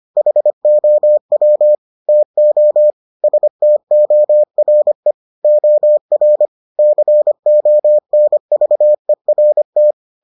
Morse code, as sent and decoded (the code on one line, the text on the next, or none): .... --- .-- / - --- / ... - --- .-. / --- .-. / -.-. --- -. ...- . .-. -
HOW TO STORE OR CONVERT